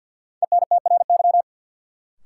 Morse code, heard as a keyed tone, about 52 wpm, 700 Hz. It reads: ENTRY
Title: Sentences from Top 1500 Words - New Word Drill - 50wpm